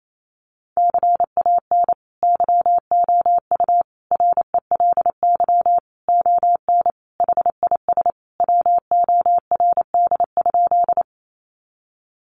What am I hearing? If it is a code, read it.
Morse code, heard as a keyed tone, about 28 wpm, 700 Hz. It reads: CAN YOU RELY ON HIS WORD?
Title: Sentences from Top 1200 Words 28wpm